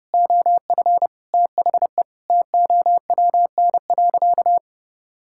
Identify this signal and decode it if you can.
Morse code, heard as a keyed tone, about 30 wpm, 700 Hz. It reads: OF THE TOWN.
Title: Sentences from Top 1000 Words - Review Part 5 - 30wpm